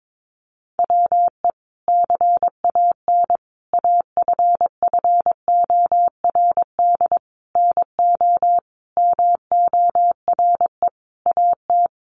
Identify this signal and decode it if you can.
Morse code, heard as a keyed tone, about 22 wpm, 700 Hz. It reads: WE CAN AFFORD NO MORE AT
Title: Sentences from Top 1100 Words 22wpm